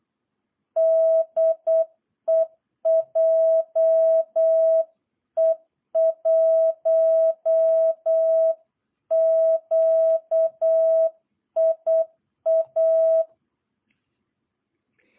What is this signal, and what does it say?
Morse code, heard as a keyed tone, about 8 wpm, 660 Hz. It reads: DE JE1QIA
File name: jh8cbh / 2022cw